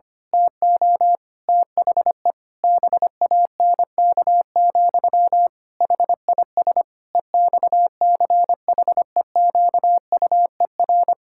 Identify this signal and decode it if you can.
Morse code, heard as a keyed tone, about 25 wpm, 700 Hz. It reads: TO THE BANK, HIS EXCHEQUER
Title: Pangram Challenge v2 - 25wpm